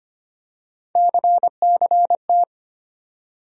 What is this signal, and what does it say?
Morse code, heard as a keyed tone, about 25 wpm, 700 Hz. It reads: CCT
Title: USA Call Sign Postfixes - 3 Letters 25wpm